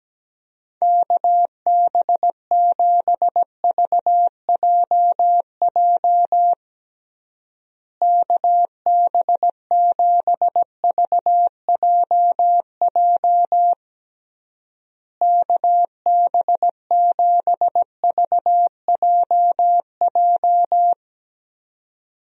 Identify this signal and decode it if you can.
Morse code, heard as a keyed tone, about 17 wpm, 700 Hz. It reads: KB7VJJ KB7VJJ KB7VJJ
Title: USA Call Signs - 6 Characters - Repeated 3x - 17wpm